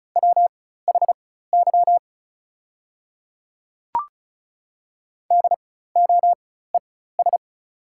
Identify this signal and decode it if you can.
Morse code, heard as a keyed tone, about 36 wpm, spaced as for 16 wpm, 690 Hz. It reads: WHY DOES
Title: 4x Character Spacing - Top 500 Words - 35wpm